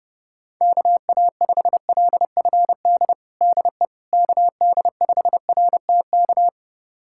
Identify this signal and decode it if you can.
Morse code, heard as a keyed tone, about 32 wpm, 700 Hz. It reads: KA5LFD DE KD5RTK
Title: USA Callsign DE Callsign - 6 Characters - 30wpm